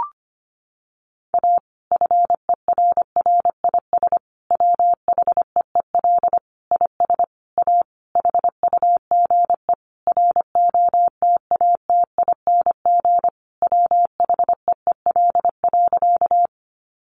Morse code, heard as a keyed tone, about 25 wpm, 700 Hz. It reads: A FERRIS WHEEL IS A HUGE ROTATING WHEEL.
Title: Additional Easy Phrases 25wpm